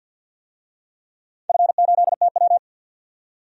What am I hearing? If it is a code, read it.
Morse code, heard as a keyed tone, about 47 wpm, 700 Hz. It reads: F8TW
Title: DXCC-100 Call Signs - 4 Characters - 50wpm